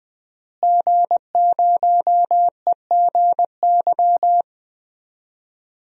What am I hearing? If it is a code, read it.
Morse code, heard as a keyed tone, about 21 wpm, 700 Hz. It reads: G0EGY
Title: DXCC-100 Call Signs - 5 Characters - Repeated 3x - 20wpm